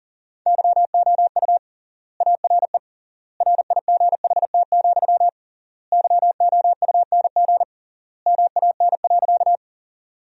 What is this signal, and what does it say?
Morse code, heard as a keyed tone, about 40 wpm, 700 Hz. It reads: YOU ARE RIGHT, YOUNG MAN.
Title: 3x Word-Spacing - Sentences from Top 400 Words - 40wpm